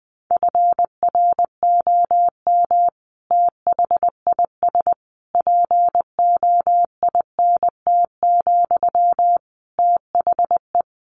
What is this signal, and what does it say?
Morse code, heard as a keyed tone, about 20 wpm, 700 Hz. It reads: FROM THIS POINT, THE
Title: Sentences from Top 1200 Words 20wpm